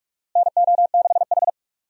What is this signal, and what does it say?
Morse code, heard as a keyed tone, about 46 wpm, 700 Hz. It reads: NO6H
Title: USA Call Signs - 4 Characters - Repeated 2x - 45wpm